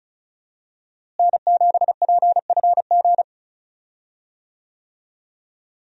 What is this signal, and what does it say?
Morse code, heard as a keyed tone, about 36 wpm, 700 Hz. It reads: N7PFG
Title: USA Call Signs - 5 Characters - Repeated 2x - 35wpm